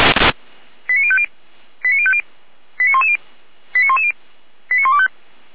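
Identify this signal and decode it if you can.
Five-tone selective call (ZVEI), heard as quick runs of five tones, tones of about 70 ms. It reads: 89050 89050 89110 89110 89125
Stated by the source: ZVEI five-tone call received by SDR, from the Feuerwehr Orlamünde audio archive.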